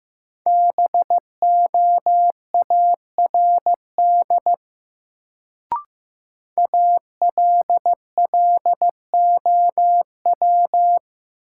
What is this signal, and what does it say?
Morse code, heard as a keyed tone, about 15 wpm, 700 Hz. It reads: BOARD ALLOW
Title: Sentences from Top 700 Words - New Word Drill - 15wpm